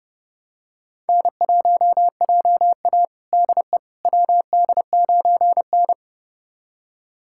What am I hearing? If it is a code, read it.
Morse code, heard as a keyed tone, about 32 wpm, 700 Hz. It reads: N1JA DE WD9N
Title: USA Callsign DE Callsign - 4 Characters - 30wpm